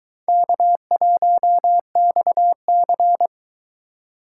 Morse code, heard as a keyed tone, about 22 wpm, 700 Hz. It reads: K1XC